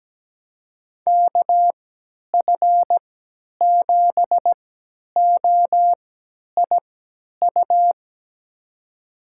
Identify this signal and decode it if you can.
Morse code, heard as a keyed tone, about 17 wpm, spaced as for 10 wpm, 700 Hz. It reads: KF7OIU
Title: USA Call Signs - 6 Characters - 3x Character Spacing - 17wpm